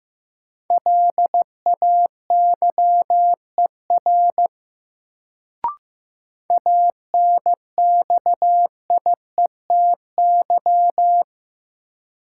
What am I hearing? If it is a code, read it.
Morse code, heard as a keyed tone, about 15 wpm, 700 Hz. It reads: LAYER ANXIETY
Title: Sentences from Top 1600 Words - New Word Drill - 15wpm